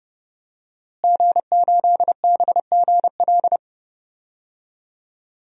Morse code, heard as a keyed tone, about 31 wpm, 700 Hz. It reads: G8BGL
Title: DXCC-100 Call Signs - 5 Characters - Repeated 3x - 30wpm